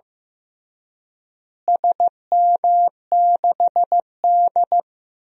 Morse code, heard as a keyed tone, about 15 wpm, 700 Hz. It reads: SM6D
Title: DXCC-100 Call Signs - 4 Characters - Repeated 2x - 15wpm